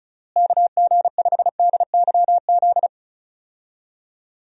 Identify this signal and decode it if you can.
Morse code, heard as a keyed tone, about 34 wpm, 700 Hz. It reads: KG5DYZ
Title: USA Call Signs - 6 Characters - Repeated 2x - 35wpm